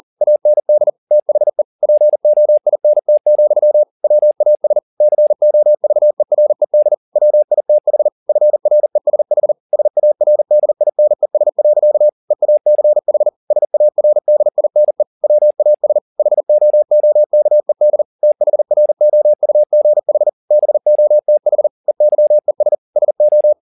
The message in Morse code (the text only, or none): AND THE POINT, WAS COVERED WITH FRESH SARDINES. EACH SARDINE WAS HOOKED THROUGH BOTH EYES SO